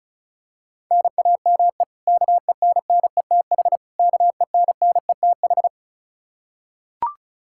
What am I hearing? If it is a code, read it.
Morse code, heard as a keyed tone, about 34 wpm, 700 Hz. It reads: NAME KENNETH KENNETH
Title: Practice QSO - Exchange 1 - Beginning - 35wpm